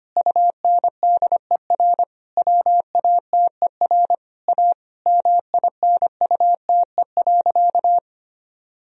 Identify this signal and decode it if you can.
Morse code, heard as a keyed tone, about 25 wpm, 700 Hz. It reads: UNDER WATER A MINUTE.
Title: Sentences from Top 1000 Words - Review Part 1 - 25wpm